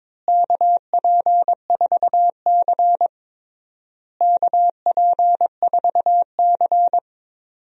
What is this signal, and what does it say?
Morse code, heard as a keyed tone, about 22 wpm, 700 Hz. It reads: KP4C KP4C
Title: DXCC-100 Call Signs - 4 Characters - Repeated 3x - 22wpm